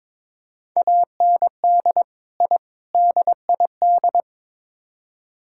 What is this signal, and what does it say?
Morse code, heard as a keyed tone, about 22 wpm, 700 Hz. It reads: AND I DID
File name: practice / 22wpm / Build A Phrase - 5-Word Phrases - 22wpm